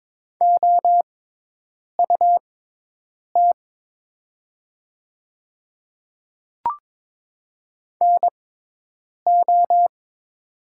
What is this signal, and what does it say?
Morse code, heard as a keyed tone, about 22 wpm, spaced as for 8 wpm, 700 Hz. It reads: OUT NO
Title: 6x Character Spacing - Top 100 Words - 22wpm